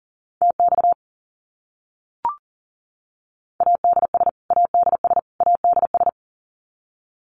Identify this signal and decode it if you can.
Morse code, heard as a keyed tone, about 40 wpm, 700 Hz. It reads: TX ADS ADS ADS
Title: Abbreviations Repeated 3x - 40wpm